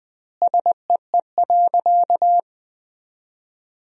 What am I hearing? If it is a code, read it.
Morse code, heard as a keyed tone, about 20 wpm, 700 Hz. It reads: SEE.
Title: 2x Word-Spacing - Sentences from Top 500 Words - 20wpm